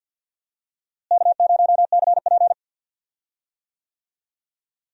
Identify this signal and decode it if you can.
Morse code, heard as a keyed tone, about 46 wpm, 700 Hz. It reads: K0CP
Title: USA Call Signs - 4 Characters - Repeated 3x - 50wpm